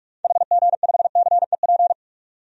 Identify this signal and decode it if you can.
Morse code, heard as a keyed tone, about 47 wpm, 700 Hz. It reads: HG5CEP
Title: DXCC-100 Call Signs - 6 Characters - 45wpm